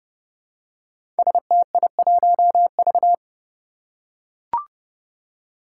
Morse code, heard as a keyed tone, about 30 wpm, 700 Hz. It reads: STI1V
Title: Five Challenging Letter-Number Combinations 30wpm